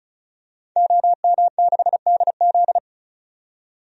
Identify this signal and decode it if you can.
Morse code, heard as a keyed tone, about 34 wpm, 700 Hz. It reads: OM6DZ
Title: DXCC-100 Call Signs - 5 Characters - 35wpm